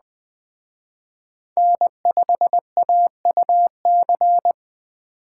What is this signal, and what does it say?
Morse code, heard as a keyed tone, about 21 wpm, 700 Hz. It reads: N5AUC